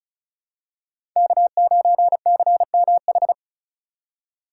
Morse code, heard as a keyed tone, about 36 wpm, 700 Hz. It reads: K9CMH
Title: USA Call Signs - 2x Character Spacing - 35wpm